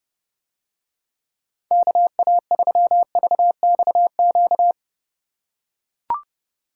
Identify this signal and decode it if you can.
Morse code, heard as a keyed tone, about 29 wpm, 700 Hz. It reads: KA3VXQ